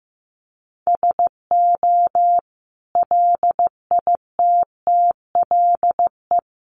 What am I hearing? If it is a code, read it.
Morse code, heard as a keyed tone, about 15 wpm, 700 Hz. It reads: SO LITTLE
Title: Sentences from Top 1500 Words 15wpm